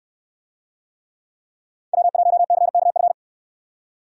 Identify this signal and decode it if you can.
Morse code, heard as a keyed tone, about 64 wpm, 700 Hz. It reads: W1CKF